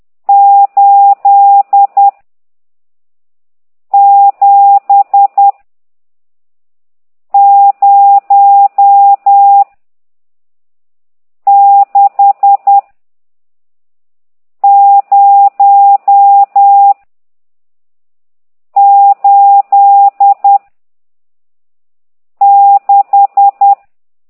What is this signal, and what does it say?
Morse code, heard as a keyed tone, about 10 wpm, 800 Hz. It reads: 8 7 0 6 0 8 6
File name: ct2idw / exerc14